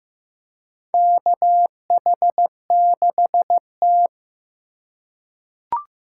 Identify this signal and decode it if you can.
Morse code, heard as a keyed tone, about 15 wpm, 700 Hz. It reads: KH6T
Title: USA Call Signs - 4 Characters - 2x Character Spacing - 15wpm